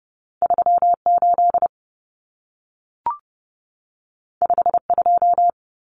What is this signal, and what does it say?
Morse code, heard as a keyed tone, about 30 wpm, 700 Hz. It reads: 38 52
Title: Numbers - 2-Digit 30wpm